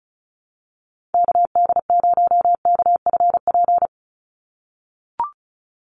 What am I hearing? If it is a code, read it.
Morse code, heard as a keyed tone, about 34 wpm, 700 Hz. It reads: KD0KFP
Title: USA Call Signs 35wpm